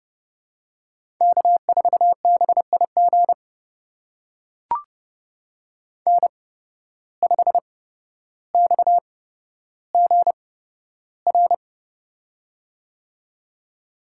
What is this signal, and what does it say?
Morse code, heard as a keyed tone, about 29 wpm, spaced as for 13 wpm, 700 Hz. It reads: K4BIG N5XGR
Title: USA Call Signs - 5 Characters - 8x Character Spacing - 30wpm